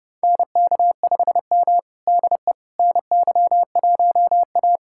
Morse code, heard as a keyed tone, about 30 wpm, 700 Hz. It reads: NK5M DE NY1A
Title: USA Callsign DE Callsign - 4 Characters - 30wpm